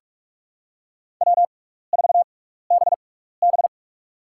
Morse code, heard as a keyed tone, about 42 wpm, spaced as for 15 wpm, 700 Hz. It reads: W4BB
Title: USA Call Signs - 6x Character Spacing - 45wpm